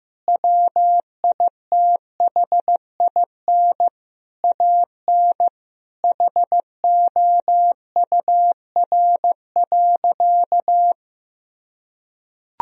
Morse code, heard as a keyed tone, about 15 wpm, 700 Hz. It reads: WITHIN AN HOUR.
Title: Sentences from Top 1000 Words - Review Part 4 - 15wpm